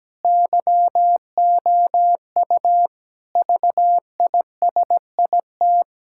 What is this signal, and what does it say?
Morse code, heard as a keyed tone, about 17 wpm, 700 Hz. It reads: YOU VISIT
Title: Sentences from Top 1000 Words - Review Part 6 - 17wpm